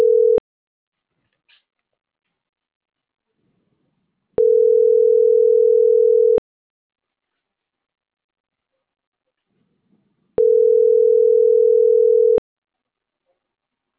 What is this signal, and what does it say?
Ringback tone (North America), 440 + 480 Hz.